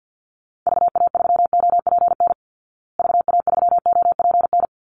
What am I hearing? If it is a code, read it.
Morse code, heard as a keyed tone, about 51 wpm, 700 Hz. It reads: VA3OPN VA3OPN
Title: Call Signs Repeated 3x - 50wpm